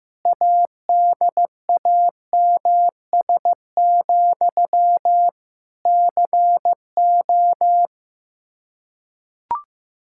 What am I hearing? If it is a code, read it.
Morse code, heard as a keyed tone, about 15 wpm, 700 Hz. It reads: ADAMS, CO
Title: Practice QSO - Exchange 1 - Beginning - 15wpm